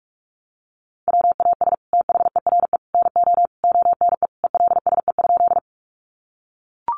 Morse code, heard as a keyed tone, about 44 wpm, 700 Hz. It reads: WAS THERE NO ONE ELSE?
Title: Sentences from Top 800 Words 45wpm